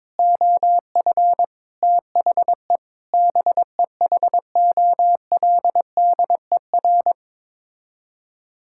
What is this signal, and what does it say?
Morse code, heard as a keyed tone, about 22 wpm, 700 Hz. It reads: OF THE BEHOLDER
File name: practice / 22wpm / Sets of 8 Words - 22wpm